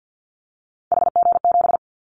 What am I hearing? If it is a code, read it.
Morse code, heard as a keyed tone, about 50 wpm, 700 Hz. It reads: HG7
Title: Callsign Prefixes for Top 25 DXCC - 50wpm